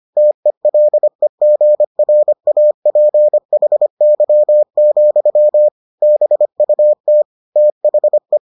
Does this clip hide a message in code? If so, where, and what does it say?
Yes: Morse code from 0.2 to 8.4 s, heard as a keyed tone, about 25 wpm, 600 Hz. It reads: TELEGRAPHY, BUT THE